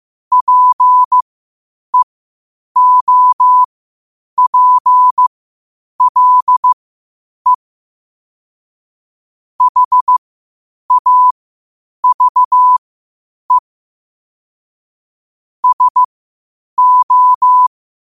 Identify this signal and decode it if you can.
Morse code, heard as a keyed tone, about 15 wpm, spaced as for 8 wpm, 1000 Hz. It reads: PEOPLE HAVE SO
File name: qotdfiles / 8wpm